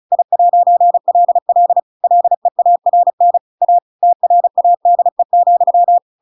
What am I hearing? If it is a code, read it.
Morse code, heard as a keyed tone, about 35 wpm, 700 Hz. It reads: I'LL LEARN A TRADE,
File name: Pinocchio / Pinocchio ch8 35 35 ik1hge